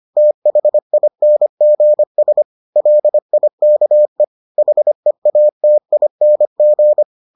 Morse code, heard as a keyed tone, about 25 wpm, 600 Hz. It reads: THINGS LIKE HEATING